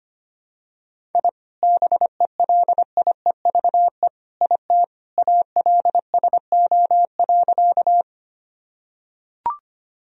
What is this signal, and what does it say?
Morse code, heard as a keyed tone, about 25 wpm, 700 Hz. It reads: I BELIEVE IT ALSO.
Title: Sentences from Top 1000 Words - Review Part 1 - 25wpm